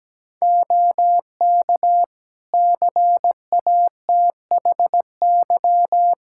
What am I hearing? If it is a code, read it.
Morse code, heard as a keyed tone, about 17 wpm, 700 Hz. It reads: OK CATHY
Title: Practice QSO - Exchange 4 - Ending - 17wpm